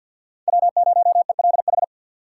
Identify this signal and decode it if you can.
Morse code, heard as a keyed tone, about 46 wpm, 700 Hz. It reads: W0ELH
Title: USA Call Signs - 2x Character Spacing - 50wpm